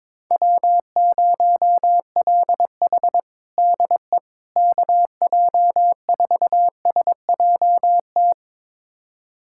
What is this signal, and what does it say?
Morse code, heard as a keyed tone, about 22 wpm, 700 Hz. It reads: W0LH DE KJ4SJT
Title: USA Callsign DE Callsign 22wpm